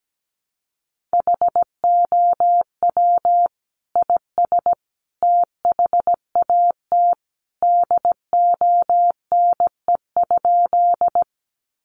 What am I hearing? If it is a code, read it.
Morse code, heard as a keyed tone, about 17 wpm, 700 Hz. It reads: HOW IS THAT DONE?
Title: Sentences from Top 500 Words 17wpm